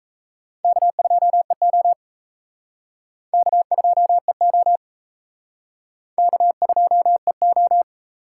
Morse code, heard as a keyed tone, about 34 wpm, 700 Hz. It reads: K2EO K2EO K2EO
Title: Speed Racing - USA 1x2 Call Signs 28wpm